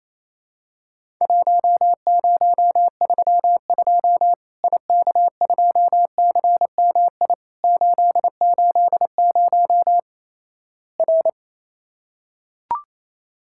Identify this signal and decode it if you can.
Morse code, heard as a keyed tone, about 28 wpm, 700 Hz. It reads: 1032 IK2CMI 880 R
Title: Worked All Europe DX Contest - QTC Exchange - 28wpm